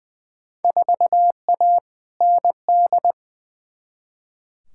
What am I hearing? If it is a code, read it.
Morse code, heard as a keyed tone, about 20 wpm, 700 Hz. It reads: 4A ND
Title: Field Day Exchanges Encore - 20wpm